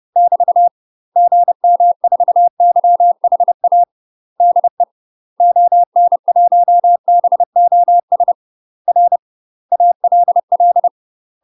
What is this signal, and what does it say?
Morse code, heard as a keyed tone, about 30 wpm, 700 Hz. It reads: <BT> GM4YHA DE ON1BOS R ALL
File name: UK MASTER QSOs TRAINING / 30WPM / UK MASTER QSOs 0014